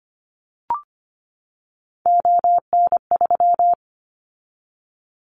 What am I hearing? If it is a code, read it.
Morse code, heard as a keyed tone, about 25 wpm, 700 Hz. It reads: ON3
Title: Callsign Prefixes for Top 100 DXCC - 25wpm